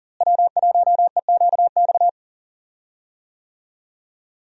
Morse code, heard as a keyed tone, about 40 wpm, 700 Hz. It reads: W1EQX
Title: USA Call Signs - 5 Characters - Repeated 3x - 40wpm